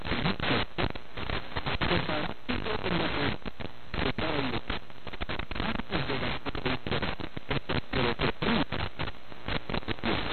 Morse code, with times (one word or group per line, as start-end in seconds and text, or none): none